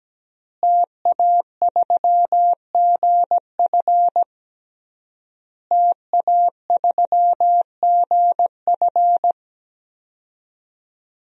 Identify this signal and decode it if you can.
Morse code, heard as a keyed tone, about 17 wpm, 700 Hz. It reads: TA3GF TA3GF
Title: DXCC-100 Call Signs - 5 Characters - Repeated 3x - 17wpm